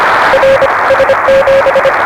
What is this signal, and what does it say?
Morse code, heard as a keyed tone, about 25 wpm, 560 Hz, with noise. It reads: RS7
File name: RS7 Beacon 16 Feb 1985